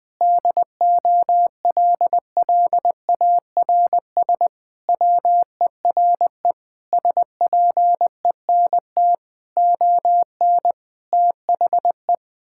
Morse code, heard as a keyed tone, about 21 wpm, 700 Hz. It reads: DOLLARS WERE SPENT ON THE
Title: Sentences with 5-Word Phrases - Spoken Beginning - 20wpm